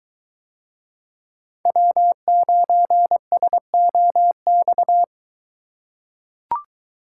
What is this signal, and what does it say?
Morse code, heard as a keyed tone, about 23 wpm, 700 Hz. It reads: W9SOX